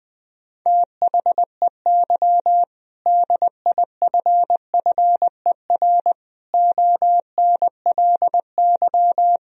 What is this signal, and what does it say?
Morse code, heard as a keyed tone, about 21 wpm, 700 Hz. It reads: THEY DIFFER ONLY
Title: Sentences from Top 1000 Words - Review Part 2 - 20wpm